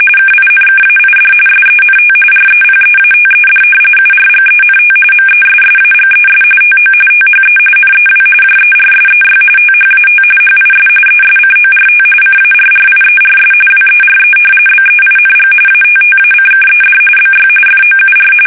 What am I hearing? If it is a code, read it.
RTTY, about 45.45 baud. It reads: KINKAID SENT REPEATED URGENT MESSAGES TO THE NEW JERSEY THAT HIS ESCORT CARRIERS WERE BEING ATTACKED BY FOUR
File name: fsk850 / battleshipnj